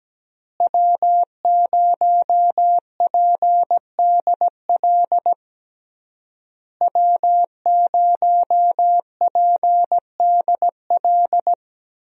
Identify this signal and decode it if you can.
Morse code, heard as a keyed tone, about 17 wpm, 700 Hz. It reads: W0PDL W0PDL
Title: USA Call Signs - 5 Characters - Repeated 3x - 17wpm